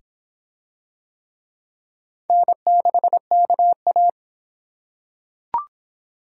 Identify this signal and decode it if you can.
Morse code, heard as a keyed tone, about 26 wpm, 700 Hz. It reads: N6KA